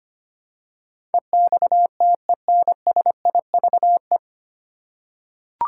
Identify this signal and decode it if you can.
Morse code, heard as a keyed tone, about 25 wpm, 700 Hz. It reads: EXTENSIVE
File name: Nine Letter Words - 25wpm